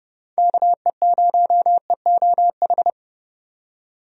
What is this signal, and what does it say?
Morse code, heard as a keyed tone, about 30 wpm, 700 Hz. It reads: KE0EOH
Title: USA Call Signs - 6 Characters - 30wpm